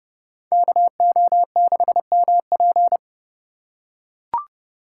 Morse code, heard as a keyed tone, about 28 wpm, 700 Hz. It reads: KO6MP